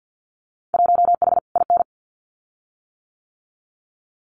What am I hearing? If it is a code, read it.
Morse code, heard as a keyed tone, about 50 wpm, 700 Hz. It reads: 1H IN
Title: Winter Field Day Exchanges 50wpm